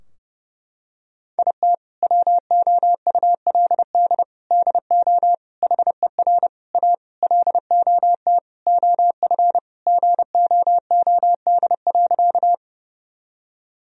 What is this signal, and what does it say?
Morse code, heard as a keyed tone, about 30 wpm, 700 Hz. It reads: IT WOULD DO HER A LOT OF GOOD.
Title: Sentences from Top 1000 Words - Review Part 3 - 30wpm